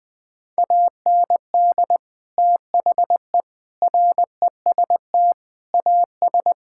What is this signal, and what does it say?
Morse code, heard as a keyed tone, about 20 wpm, 700 Hz. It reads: AND THE REST AS
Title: Sets of 8 Words - 20wpm